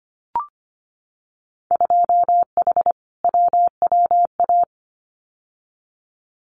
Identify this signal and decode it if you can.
Morse code, heard as a keyed tone, about 25 wpm, 700 Hz. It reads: 2H WWA
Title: Winter Field Day Exchanges 25wpm